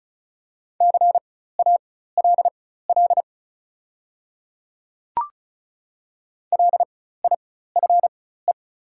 Morse code, heard as a keyed tone, about 36 wpm, spaced as for 16 wpm, 690 Hz. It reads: CALL LIFE
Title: 4x Character Spacing - Top 300 Words - 35wpm